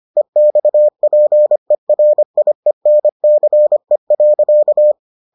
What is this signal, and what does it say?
Morse code, heard as a keyed tone, about 25 wpm, 600 Hz. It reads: EXPERIENCE.